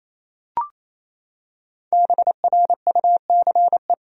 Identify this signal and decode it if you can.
Morse code, heard as a keyed tone, about 28 wpm, 700 Hz. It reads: BRUCE
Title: USA Common Names - 28wpm